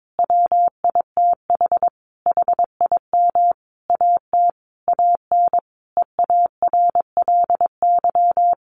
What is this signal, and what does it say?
Morse code, heard as a keyed tone, about 22 wpm, 700 Hz. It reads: WITH HIM AT AN EARLY
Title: Sentences from Top 1200 Words 22wpm